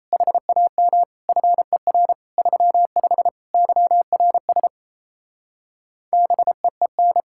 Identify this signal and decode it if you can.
Morse code, heard as a keyed tone, about 37 wpm, 700 Hz. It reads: HAM FER 35 YRS BEEN